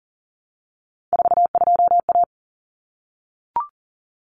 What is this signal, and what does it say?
Morse code, heard as a keyed tone, about 40 wpm, 700 Hz. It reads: 42A
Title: Common Cut Numbers - 3-Digit 40wpm